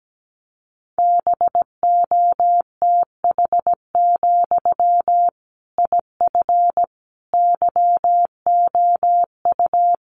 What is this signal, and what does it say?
Morse code, heard as a keyed tone, about 17 wpm, 700 Hz. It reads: BOTH, IF YOU
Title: Sentences from Top 1500 Words 17wpm